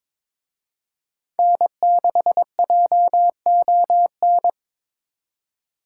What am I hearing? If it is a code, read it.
Morse code, heard as a keyed tone, about 26 wpm, 700 Hz. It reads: N6JON
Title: Speed Racing - USA Call Signs 22wpm